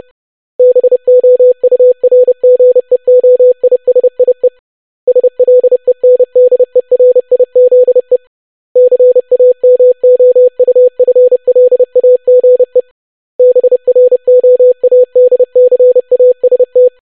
Morse code, heard as a keyed tone, about 30 wpm, 500 Hz. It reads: BOURGEOISIE SLENDERIZE CAMOUFLAGE BROADCAST
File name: longwords-25wpm